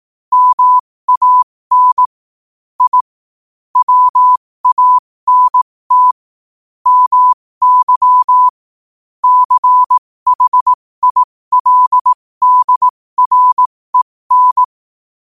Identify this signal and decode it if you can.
Morse code, heard as a keyed tone, about 18 wpm, spaced as for 15 wpm, 1000 Hz. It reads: MAN I WANT MY CHILDREN
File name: qotdfiles / 15wpm